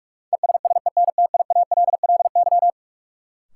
Morse code, heard as a keyed tone, about 46 wpm, 700 Hz. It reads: ESSENTIALLY
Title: Sentences from Top 1300 Words - New Word Drill Alt - 45wpm